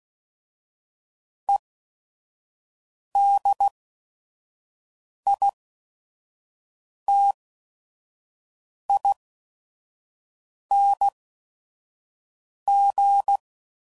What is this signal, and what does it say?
Morse code, heard as a keyed tone, about 16 wpm, spaced as for 5 wpm, 800 Hz. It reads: EDITING
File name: Suffixes / Special ING's / VerySlowToFaster ING's